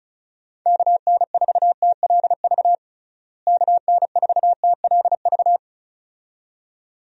Morse code, heard as a keyed tone, about 35 wpm, 700 Hz. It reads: KN4TLV KN4TLV
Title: USA Call Signs - 6 Characters - Repeated 2x - 35wpm